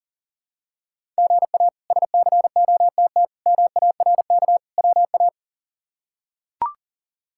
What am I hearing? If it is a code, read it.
Morse code, heard as a keyed tone, about 40 wpm, 700 Hz. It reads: GA SCOTT MARK WA
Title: K1USN SST QSO Practice - 40wpm